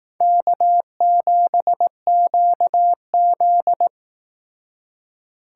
Morse code, heard as a keyed tone, about 19 wpm, 700 Hz. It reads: K7QZ
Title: Speed Racing - USA 1x2 Call Signs 15wpm